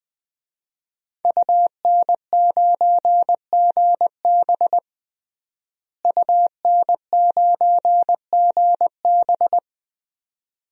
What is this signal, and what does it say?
Morse code, heard as a keyed tone, about 20 wpm, 700 Hz. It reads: UN9GB UN9GB
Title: DXCC-100 Call Signs - 5 Characters - Repeated 2x - 20wpm